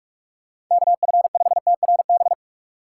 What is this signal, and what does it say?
Morse code, heard as a keyed tone, about 46 wpm, 700 Hz. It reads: KF5TRB
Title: USA Call Signs - 6 Characters - 6x Character Spacing - 45wpm